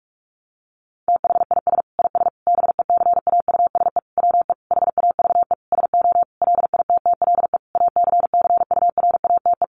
Morse code, heard as a keyed tone, about 44 wpm, 700 Hz. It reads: THIS IS BECAUSE WE HAVE SO LITTLE ACCURATE
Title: Sentences from Top 1200 Words 45wpm